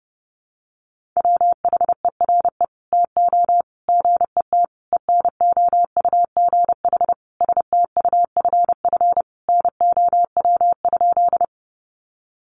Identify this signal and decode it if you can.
Morse code, heard as a keyed tone, about 30 wpm, 700 Hz. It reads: WHERE TO GET ENOUGH STUFF NOW?
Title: Sentences from Top 1400 Words 30wpm